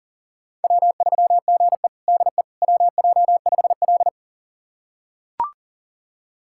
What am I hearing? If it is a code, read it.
Morse code, heard as a keyed tone, about 40 wpm, 700 Hz. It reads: W3GE DE WJ5L
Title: USA Callsign DE Callsign - 4 Characters - 40wpm